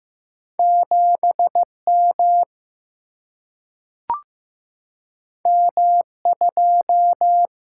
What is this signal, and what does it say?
Morse code, heard as a keyed tone, about 15 wpm, 700 Hz. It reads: 7 M M 2
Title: Single-Letter Single-Number 15wpm